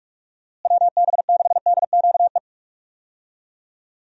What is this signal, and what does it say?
Morse code, heard as a keyed tone, about 46 wpm, 700 Hz. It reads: WB6DQE
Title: USA Call Signs - 6 Characters - Repeated 2x - 45wpm